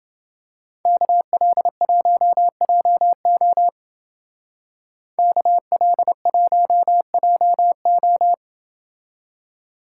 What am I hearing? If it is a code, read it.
Morse code, heard as a keyed tone, about 28 wpm, 700 Hz. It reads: KL1JO KL1JO